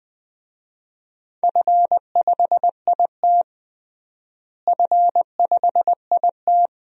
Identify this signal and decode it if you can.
Morse code, heard as a keyed tone, about 20 wpm, 700 Hz. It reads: F5IT F5IT
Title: DXCC-100 Call Signs - 4 Characters - Repeated 2x - 20wpm